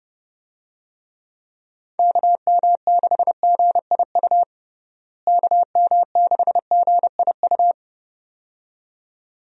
Morse code, heard as a keyed tone, about 32 wpm, 700 Hz. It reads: KM6GIU KM6GIU
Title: USA Call Signs - 6 Characters - Repeated 2x - 30wpm